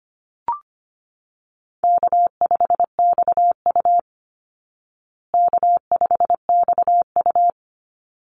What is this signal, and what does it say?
Morse code, heard as a keyed tone, about 25 wpm, 700 Hz. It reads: K5XU K5XU
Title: Call Signs Repeated 3x - 25wpm